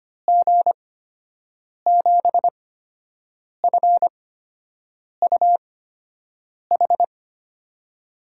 Morse code, heard as a keyed tone, about 25 wpm, spaced as for 7 wpm, 700 Hz. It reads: G7FUH
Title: USA Call Signs - 6 Characters - 8x Character Spacing - 25wpm